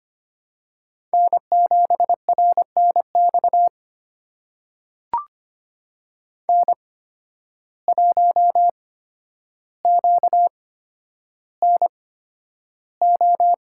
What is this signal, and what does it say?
Morse code, heard as a keyed tone, about 25 wpm, spaced as for 11 wpm, 700 Hz. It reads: N7RNX N1QNO
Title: USA Call Signs - 5 Characters - 8x Character Spacing - 25wpm